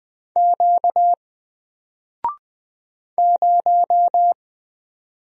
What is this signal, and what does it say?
Morse code, heard as a keyed tone, about 20 wpm, 700 Hz. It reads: Q 0
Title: Single Letter-Number Alt 20wpm